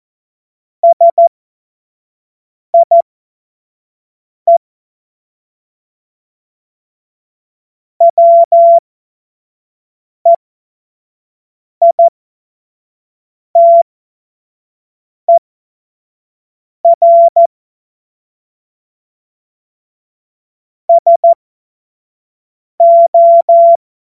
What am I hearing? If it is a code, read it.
Morse code, heard as a keyed tone, about 14 wpm, spaced as for 5 wpm, 680 Hz. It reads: SIE WEITER SO
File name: Morsekurs / Witze / Witz 13 Mai 2025 25 F70